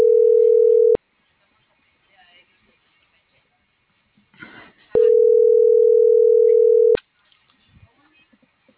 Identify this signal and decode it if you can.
Ringback tone (North America), 440 + 480 Hz.
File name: CENTRAL MABE / out manuel.posadas